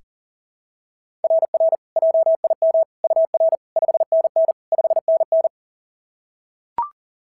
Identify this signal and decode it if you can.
Morse code, heard as a keyed tone, about 38 wpm, 660 Hz. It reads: RR JIM UR 5NN 5NN
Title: SOTA QSO Practice 40wpm